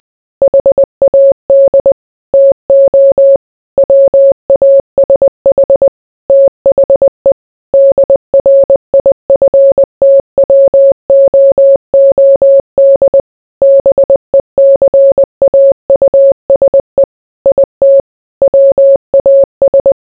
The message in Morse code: HAD TO WASH THE DRIFTWOOD BECAUSE IT WAS